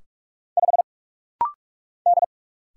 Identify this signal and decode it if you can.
Morse code, heard as a keyed tone, about 45 wpm, 700 Hz. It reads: 5 D
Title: Single Letter-Number Alt - Rapid-Fire 45wpm